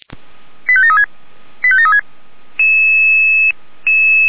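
Five-tone selective call (ZVEI), heard as quick runs of five tones, tones of about 70 ms. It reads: 85736 85736
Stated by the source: Five-tone ZVEI selective call received on an SDR.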